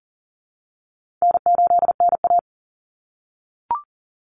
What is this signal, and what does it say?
Morse code, heard as a keyed tone, about 40 wpm, 700 Hz. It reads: N8NA
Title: Call Signs - 40wpm